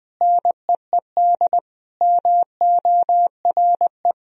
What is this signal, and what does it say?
Morse code, heard as a keyed tone, about 20 wpm, 700 Hz. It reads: NEED MORE
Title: Sentences from Top 1000 Words - Review Part 2 - 20wpm